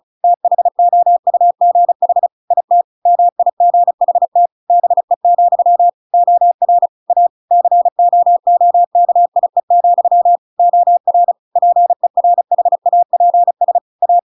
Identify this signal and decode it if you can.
Morse code, heard as a keyed tone, about 34 wpm, 700 Hz. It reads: THOUGH IT MIGHT BE, OR A COOKIE, OR PERHAPS A